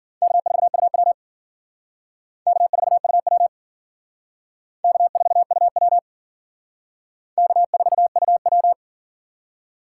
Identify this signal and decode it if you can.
Morse code, heard as a keyed tone, about 52 wpm, 700 Hz. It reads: K4UW K4UW K4UW K4UW